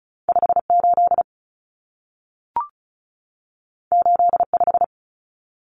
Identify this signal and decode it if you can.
Morse code, heard as a keyed tone, about 36 wpm, 700 Hz. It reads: 58 85
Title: Common Cut Numbers - 2-Digit 35wpm